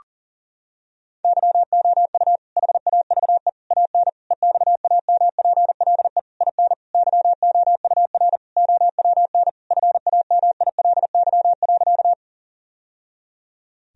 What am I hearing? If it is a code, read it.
Morse code, heard as a keyed tone, about 40 wpm, 700 Hz. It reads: YOU HAVE AN EXAMPLE IN YOUR OWN FAMILY.
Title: Sentences from Top 1000 Words - Review Part 1 - 40wpm